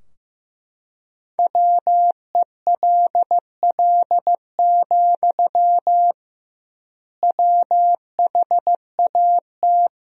Morse code, heard as a keyed tone, about 15 wpm, 700 Hz. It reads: WELL, WHAT
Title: 2x Word-Spacing - Sentences from Top 500 Words - 15wpm